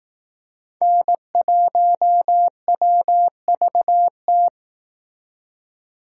Morse code, heard as a keyed tone, about 19 wpm, 700 Hz. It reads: N1WVT
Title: Speed Racing - USA 1x3 Call Signs 15wpm